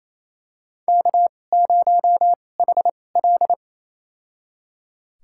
Morse code, heard as a keyed tone, about 28 wpm, spaced as for 20 wpm, 700 Hz. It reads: K0HL
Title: USA Call Signs - 4 Characters - 2x Character Spacing - 28wpm